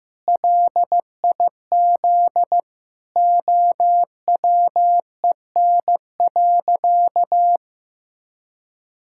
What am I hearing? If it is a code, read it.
Morse code, heard as a keyed tone, about 16 wpm, 700 Hz. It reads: LIZ OWEN.
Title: Pangram Challenge v2 - 15wpm